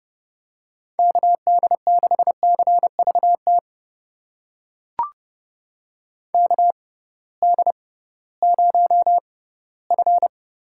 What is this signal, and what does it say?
Morse code, heard as a keyed tone, about 30 wpm, spaced as for 17 wpm, 700 Hz. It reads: KD6CVT KD0F
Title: USA Call Signs - 6 Characters - 6x Character Spacing - 30wpm